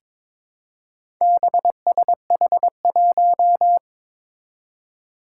Morse code, heard as a keyed tone, about 22 wpm, 700 Hz. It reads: BSH1
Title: Four Challenging Letter-Number Combinations 22wpm